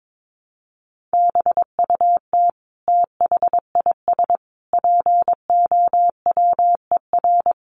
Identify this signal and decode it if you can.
Morse code, heard as a keyed tone, about 22 wpm, 700 Hz. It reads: BUT THIS POWER
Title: Sentences from Top 1500 Words 22wpm